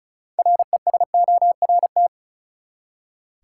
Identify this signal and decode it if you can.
Morse code, heard as a keyed tone, about 34 wpm, 700 Hz. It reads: RESORT
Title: Six Letter Words - 35wpm